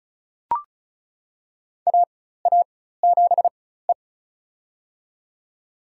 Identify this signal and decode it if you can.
Morse code, heard as a keyed tone, about 36 wpm, spaced as for 16 wpm, 700 Hz. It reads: AA7E
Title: USA Call Signs - 4 Characters - 4x Character Spacing - 35wpm